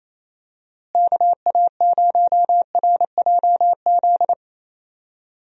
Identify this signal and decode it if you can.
Morse code, heard as a keyed tone, about 28 wpm, 700 Hz. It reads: KA0RJZ
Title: USA Call Signs - 6 Characters - Repeated 3x - 28wpm